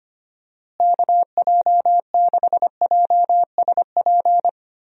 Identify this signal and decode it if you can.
Morse code, heard as a keyed tone, about 25 wpm, 700 Hz. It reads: KJ6JSP
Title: USA Call Signs - 6 Characters - Repeated 2x - 25wpm